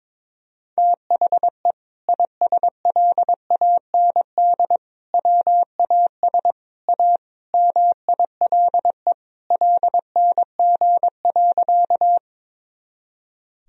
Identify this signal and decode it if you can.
Morse code, heard as a keyed tone, about 23 wpm, 700 Hz. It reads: THE ISLAND WAS A MILE LNG.
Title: Sentences from Top 500 Words and Abbreviations 22wpm